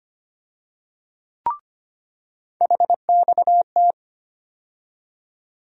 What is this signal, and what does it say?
Morse code, heard as a keyed tone, about 25 wpm, 700 Hz. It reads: HXT
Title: USA Call Sign Postfixes - 3 Letters 25wpm